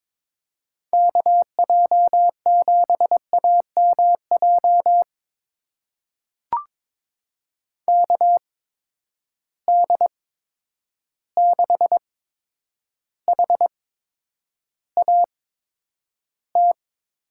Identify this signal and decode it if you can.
Morse code, heard as a keyed tone, about 22 wpm, spaced as for 9 wpm, 700 Hz. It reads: KJ7AMJ KD6HAT
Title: USA Call Signs - 6 Characters - 8x Character Spacing - 22wpm